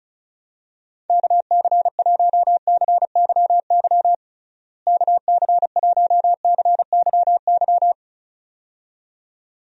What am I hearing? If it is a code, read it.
Morse code, heard as a keyed tone, about 34 wpm, 700 Hz. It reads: KC1CYY KC1CYY
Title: USA Call Signs - 6 Characters - Repeated 2x - 35wpm